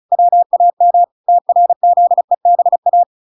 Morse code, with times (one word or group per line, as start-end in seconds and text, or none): 0.1-1.0 WAM
1.3-3.0 TRZEBA